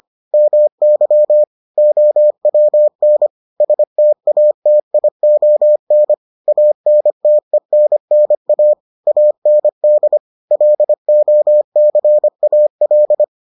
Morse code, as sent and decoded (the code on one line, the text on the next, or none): -- -.-- / --- .-- -. / ... - .- - .. --- -. / .- -. - . -. -. .- / .- -. -.. / .-.. --- -.-. .- .-..
MY OWN STATION ANTENNA AND LOCAL